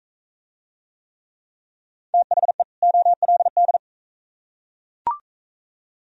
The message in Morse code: THE OLD

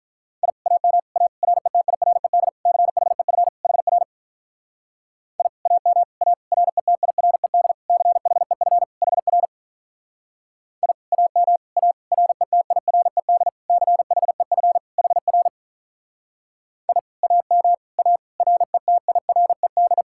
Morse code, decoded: I AM A RETIRED CHEF HR I AM A RETIRED CHEF HR I AM A RETIRED CHEF HR I AM A RETIRED

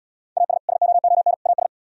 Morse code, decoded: AS SOON AS

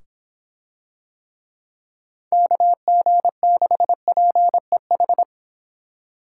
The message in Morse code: KG6PEH

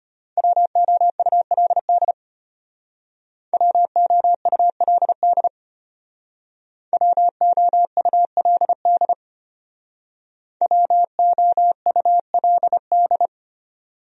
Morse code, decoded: WOULD WOULD WOULD WOULD

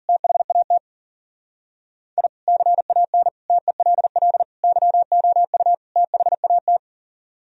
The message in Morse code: THAT I CAN TELL YOU THAT

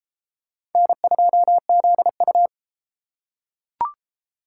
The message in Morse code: N2ZU

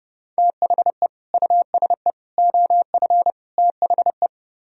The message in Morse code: THE USE OF THE

